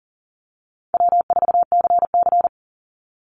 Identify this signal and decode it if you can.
Morse code, heard as a keyed tone, about 40 wpm, 700 Hz. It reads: W4CC
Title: Easy Call Signs - 40wpm